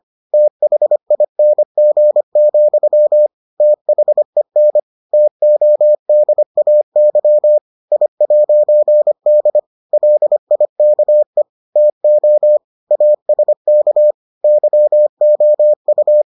Morse code, heard as a keyed tone, about 25 wpm, 600 Hz. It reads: THING, THEN TODAY I'D LIKE TO ASK YOU